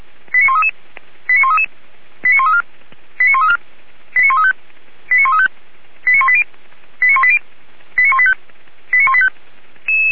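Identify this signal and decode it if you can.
Five-tone selective call (ZVEI), heard as quick runs of five tones, tones of about 70 ms. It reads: 89133 89133 89134 89134 89135 89135 89180 89180 89185 89185
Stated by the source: ZVEI five-tone selective call 89131 received on an SDR.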